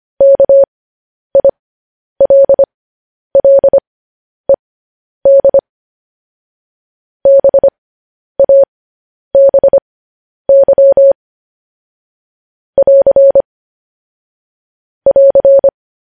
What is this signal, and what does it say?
Morse code, heard as a keyed tone, about 25 wpm, spaced as for 10 wpm, 550 Hz. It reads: KILLED BABY <AR> <AR>
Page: news headlines in morse code at 10 wpm